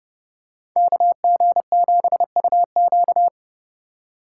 Morse code, heard as a keyed tone, about 30 wpm, 700 Hz. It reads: KG7UQ